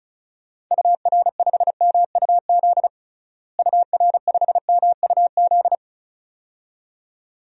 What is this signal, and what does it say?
Morse code, heard as a keyed tone, about 34 wpm, 700 Hz. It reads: UR5MUZ UR5MUZ
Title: DXCC-100 Call Signs - 6 Characters - Repeated 2x - 35wpm